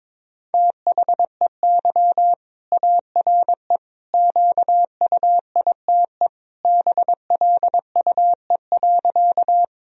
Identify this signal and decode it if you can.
Morse code, heard as a keyed tone, about 22 wpm, 700 Hz. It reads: THEY ARE QUITE BLUE.
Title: Sentences from Top 1000 Words - Review Part 3 - 22wpm